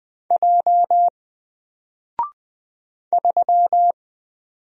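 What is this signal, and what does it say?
Morse code, heard as a keyed tone, about 20 wpm, 700 Hz. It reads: J 3
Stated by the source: Single letter and number Morse drill at 20 wpm, from N4LCD.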